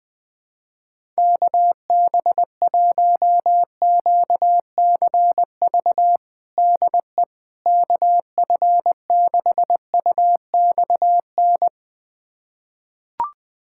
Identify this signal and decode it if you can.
Morse code, heard as a keyed tone, about 20 wpm, 700 Hz. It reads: KB1QCV DE KF6UXN
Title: USA Callsign DE Callsign - 6 Characters - 20wpm